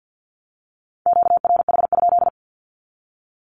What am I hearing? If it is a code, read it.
Morse code, heard as a keyed tone, about 50 wpm, 700 Hz. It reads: QRH?
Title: Q-Codes Repeated 3x 50wpm